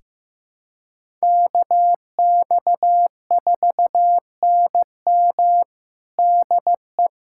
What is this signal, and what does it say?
Morse code, heard as a keyed tone, about 15 wpm, 700 Hz. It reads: KX4NM DE
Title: USA Callsign DE Callsign 15wpm